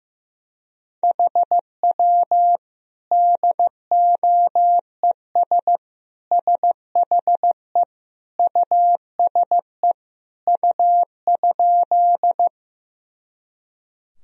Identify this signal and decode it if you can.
Morse code, heard as a keyed tone, about 16 wpm, 700 Hz. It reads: HW DOES SHE USE U?
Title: Sentences from Top 200 Words and Abbreviations 15wpm